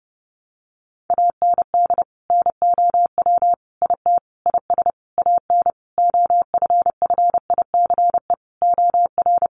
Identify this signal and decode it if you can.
Morse code, heard as a keyed tone, about 30 wpm, 700 Hz. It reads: AND NOW IT IS AN OFFICE OR
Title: Sentences from Top 1000 Words 30wpm